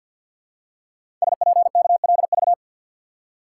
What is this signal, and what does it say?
Morse code, heard as a keyed tone, about 50 wpm, 700 Hz. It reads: SPKLV
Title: Code Groups - 5 Letters - 50wpm